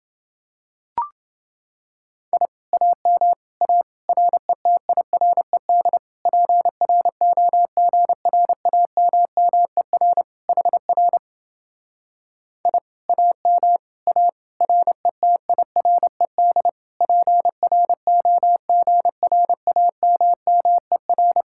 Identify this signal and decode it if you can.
Morse code, heard as a keyed tone, about 29 wpm, 700 Hz. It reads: I AM A RETIRED PROGRAMMER HR I AM A RETIRED PROGRAMMER